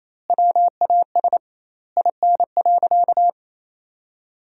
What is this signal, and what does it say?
Morse code, heard as a keyed tone, about 28 wpm, 700 Hz. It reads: WAS IN.
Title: 2x Word-Spacing - Sentences from Top 400 Words - 28wpm